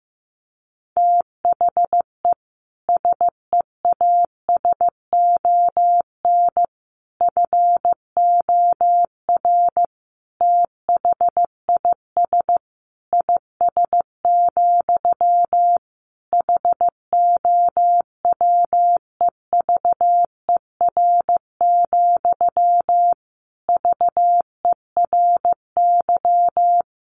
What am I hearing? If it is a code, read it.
Morse code, heard as a keyed tone, about 15 wpm, 700 Hz. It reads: THE SEASON FOR THIS IS, HOWEVER, VERY
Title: Sentences from Top 1400 Words 15wpm